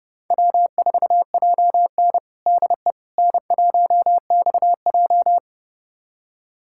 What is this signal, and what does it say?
Morse code, heard as a keyed tone, about 32 wpm, 700 Hz. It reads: W4JN DE N1XJ
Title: USA Callsign DE Callsign - 4 Characters - 30wpm